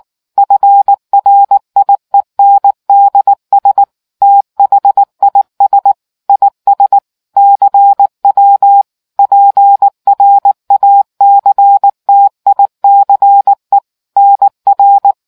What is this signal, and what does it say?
Morse code, heard as a keyed tone, about 19 wpm, 800 Hz. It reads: FRIENDS THIS IS CW PRACTICE NR